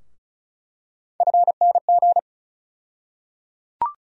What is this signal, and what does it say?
Morse code, heard as a keyed tone, about 36 wpm, 700 Hz. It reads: FNG
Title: USA Call Sign Postfixes - 3 Letters 35wpm